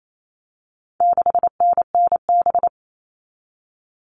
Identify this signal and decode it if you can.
Morse code, heard as a keyed tone, about 28 wpm, 700 Hz. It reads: 6NNB
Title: All Cut Numbers - 4-Digit 28wpm